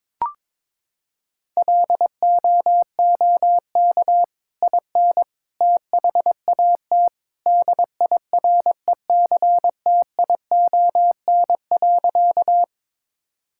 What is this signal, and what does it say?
Morse code, heard as a keyed tone, about 22 wpm, 700 Hz. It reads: LOOK IN THAT DIRECTION.
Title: Sentences from Top 1000 Words - Review Part 5 - 22wpm